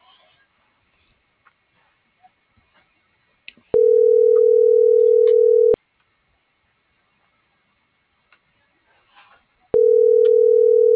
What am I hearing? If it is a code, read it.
Ringback tone (North America), 440 + 480 Hz.